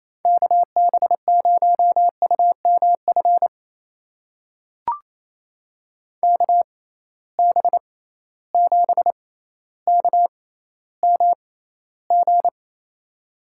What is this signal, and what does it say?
Morse code, heard as a keyed tone, about 28 wpm, spaced as for 14 wpm, 700 Hz. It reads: KB0UMF KB7KMG
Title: USA Call Signs - 6 Characters - 6x Character Spacing - 28wpm